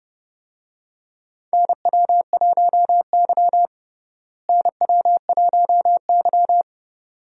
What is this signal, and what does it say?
Morse code, heard as a keyed tone, about 31 wpm, 700 Hz. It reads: NW1Y NW1Y
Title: USA Call Signs - 4 Characters - Repeated 2x - 30wpm